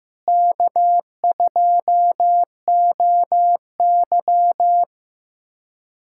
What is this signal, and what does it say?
Morse code, heard as a keyed tone, about 15 wpm, 700 Hz. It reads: K2OY
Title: USA Callsign DE Callsign - 4 Characters - 15wpm